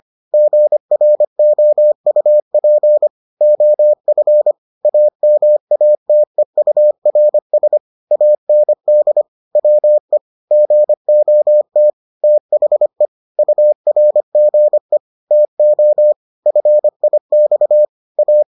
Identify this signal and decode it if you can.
Morse code, heard as a keyed tone, about 25 wpm, 600 Hz. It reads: GROUP OF AMATEURS AND WE GOT THE URGE TO FIX A